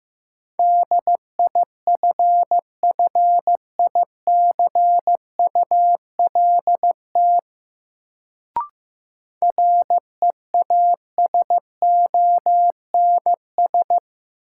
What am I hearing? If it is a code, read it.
Morse code, heard as a keyed tone, about 15 wpm, 700 Hz. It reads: DIFFICULT REASONS
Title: Sets of 4 Words - New Word Drill - 15wpm